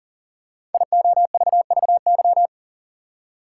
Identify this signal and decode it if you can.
Morse code, heard as a keyed tone, about 39 wpm, 700 Hz. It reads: IOVVY
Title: Code Groups - 5 Letter:Numbers - 40wpm